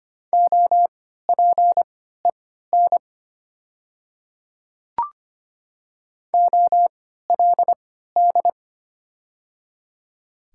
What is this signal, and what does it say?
Morse code, heard as a keyed tone, about 25 wpm, spaced as for 14 wpm, 690 Hz. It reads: OPEN OLD
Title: 3x Character Spacing - Top 500 Words - 25wpm